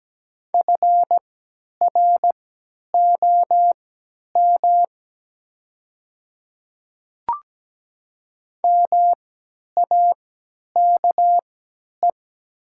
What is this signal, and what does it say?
Morse code, heard as a keyed tone, about 17 wpm, spaced as for 10 wpm, 700 Hz. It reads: FROM MAKE
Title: 3x Character Spacing - Top 100 Words - 17wpm